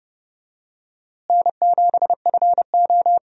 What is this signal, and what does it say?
Morse code, heard as a keyed tone, about 30 wpm, 700 Hz. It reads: N7FO
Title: USA Call Signs - 4 Characters - Repeated 2x - 30wpm